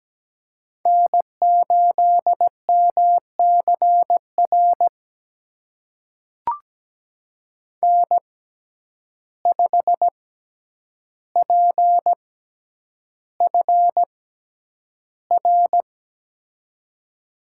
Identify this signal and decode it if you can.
Morse code, heard as a keyed tone, about 17 wpm, spaced as for 9 wpm, 700 Hz. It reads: N8MCR N5PFR
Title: USA Call Signs - 5 Characters - 6x Character Spacing - 17wpm